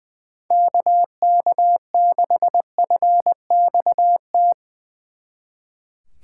Morse code, heard as a keyed tone, about 22 wpm, 700 Hz. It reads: KK6FXT
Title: Speed Racing - USA Call Signs 20wpm